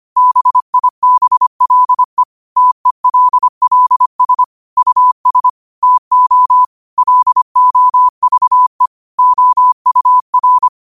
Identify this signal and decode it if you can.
Morse code, heard as a keyed tone, about 25 wpm, 1000 Hz. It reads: BIBLE TELLS US TO LOVE OUR
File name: qotdfiles / 25wpm